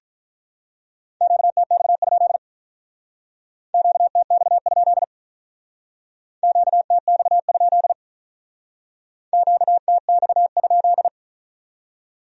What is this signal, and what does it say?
Morse code, heard as a keyed tone, about 43 wpm, 700 Hz. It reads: QTX? QTX? QTX? QTX?
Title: Speed-Racing Q-Codes - 35wpm